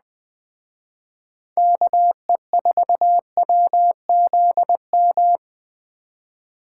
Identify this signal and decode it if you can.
Morse code, heard as a keyed tone, about 20 wpm, 700 Hz. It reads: KE4WZM